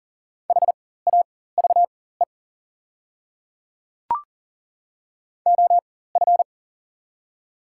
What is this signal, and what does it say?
Morse code, heard as a keyed tone, about 39 wpm, spaced as for 19 wpm, 690 Hz. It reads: HAVE OF
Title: 4x Character Spacing - Top 300 Words - 40wpm